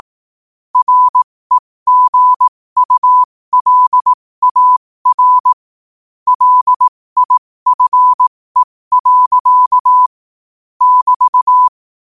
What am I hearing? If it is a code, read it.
Morse code, heard as a keyed tone, about 18 wpm, spaced as for 15 wpm, 1000 Hz. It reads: REGULAR LIFE. <BT>
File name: qotdfiles / 15wpm